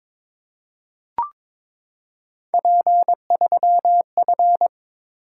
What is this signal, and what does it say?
Morse code, heard as a keyed tone, about 23 wpm, 700 Hz. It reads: P3F
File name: practice / 22wpm / Three Letter-Number Combinations - 22wpm